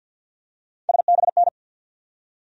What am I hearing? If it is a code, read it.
Morse code, heard as a keyed tone, about 53 wpm, 700 Hz. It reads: SBN
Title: USA Call Sign Postfixes - 3 Letters 50wpm